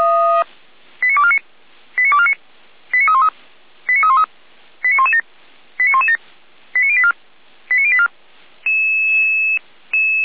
Five-tone selective call (ZVEI), heard as quick runs of five tones, tones of about 70 ms. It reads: 89249 89249 89313 89313 89117 89117 89084 89084